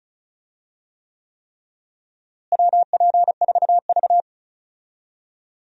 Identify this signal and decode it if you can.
Morse code, heard as a keyed tone, about 36 wpm, 700 Hz. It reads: WP4V
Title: USA Call Signs - 4 Characters - Repeated 3x - 35wpm